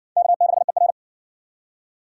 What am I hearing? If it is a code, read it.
Morse code, heard as a keyed tone, about 52 wpm, 700 Hz. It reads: K6ER